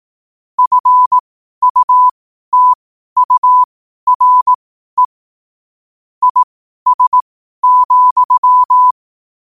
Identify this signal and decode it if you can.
Morse code, heard as a keyed tone, about 18 wpm, spaced as for 12 wpm, 1000 Hz. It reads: FUTURE IS,
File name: qotdfiles / 12wpm